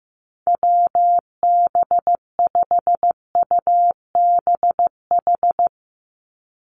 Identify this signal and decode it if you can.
Morse code, heard as a keyed tone, about 15 wpm, 700 Hz. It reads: WB5UBH
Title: USA Call Signs 15wpm